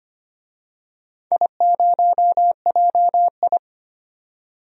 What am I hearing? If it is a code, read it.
Morse code, heard as a keyed tone, about 25 wpm, 700 Hz. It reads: I0JI
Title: DXCC-100 Call Signs - 4 Characters - 25wpm